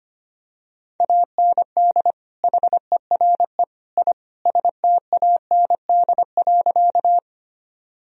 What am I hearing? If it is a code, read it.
Morse code, heard as a keyed tone, about 25 wpm, 700 Hz. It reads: AND HERE I STAND.
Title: Sentences from Top 1000 Words - Review Part 1 - 25wpm